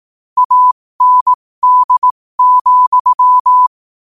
Morse code, heard as a keyed tone, about 18 wpm, spaced as for 15 wpm, 1000 Hz. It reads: AND,
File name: qotdfiles / 15wpm